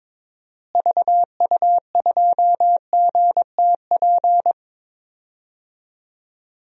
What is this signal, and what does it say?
Morse code, heard as a keyed tone, about 22 wpm, 700 Hz. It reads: VU2GTP
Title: DXCC-100 Call Signs - 6 Characters - Repeated 3x - 22wpm